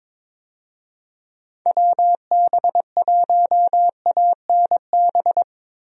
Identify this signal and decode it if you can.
Morse code, heard as a keyed tone, about 22 wpm, 700 Hz. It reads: WB1ANB
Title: USA Call Signs - 6 Characters - Repeated 3x - 22wpm